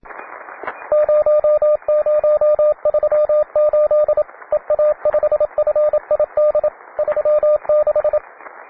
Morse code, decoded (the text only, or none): 0038 EA5FID 36